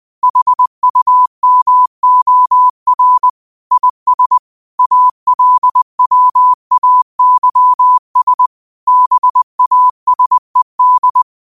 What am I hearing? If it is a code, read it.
Morse code, heard as a keyed tone, about 20 wpm, 1000 Hz. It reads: HUMOR IS ALWAYS BASED
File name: qotdfiles / 20wpm